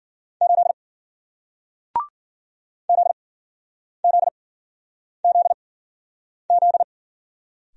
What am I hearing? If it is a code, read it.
Morse code, heard as a keyed tone, about 50 wpm, spaced as for 9 wpm, 700 Hz. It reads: 8 ZZZZ